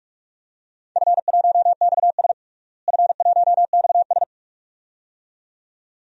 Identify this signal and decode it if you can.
Morse code, heard as a keyed tone, about 43 wpm, 700 Hz. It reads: F1XS F1XS
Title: DXCC-100 Call Signs - 4 Characters - Repeated 2x - 45wpm